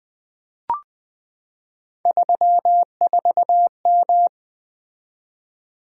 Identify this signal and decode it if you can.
Morse code, heard as a keyed tone, about 20 wpm, 700 Hz. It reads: 34M
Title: Three Letter-Number Combinations - 20wpm